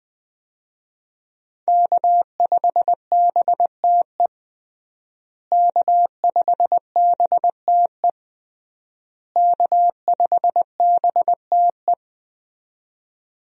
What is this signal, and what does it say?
Morse code, heard as a keyed tone, about 21 wpm, 700 Hz. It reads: K5BTE K5BTE K5BTE
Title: USA Call Signs - 5 Characters - Repeated 3x - 20wpm